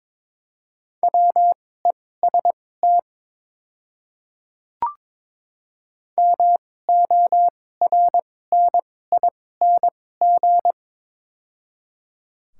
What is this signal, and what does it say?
Morse code, heard as a keyed tone, about 23 wpm, spaced as for 16 wpm, 700 Hz. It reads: WEST MORNING
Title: 2x Character Spacing - Top 500 Words - 22wpm